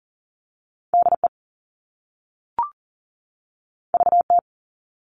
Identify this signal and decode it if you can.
Morse code, heard as a keyed tone, about 42 wpm, 700 Hz. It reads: DE VT
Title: US States 40wpm